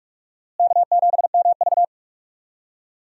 Morse code, heard as a keyed tone, about 46 wpm, 700 Hz. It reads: K7MV
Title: USA Call Signs - 4 Characters - Repeated 2x - 45wpm